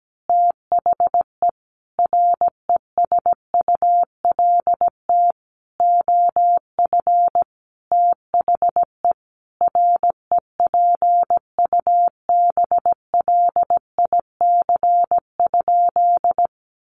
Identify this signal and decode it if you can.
Morse code, heard as a keyed tone, about 17 wpm, 700 Hz. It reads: THE RESULT OF THE REPUBLIC?
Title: Sentences from Top 1500 Words 17wpm